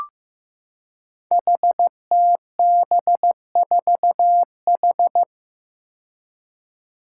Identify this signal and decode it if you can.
Morse code, heard as a keyed tone, about 15 wpm, 700 Hz. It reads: HTB4H
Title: Five Challenging Letter-Number Combinations 15wpm